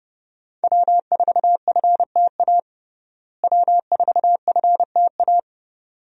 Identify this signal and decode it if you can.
Morse code, heard as a keyed tone, about 30 wpm, 700 Hz. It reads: W4FTA W4FTA
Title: USA Call Signs - 5 Characters - Repeated 2x - 30wpm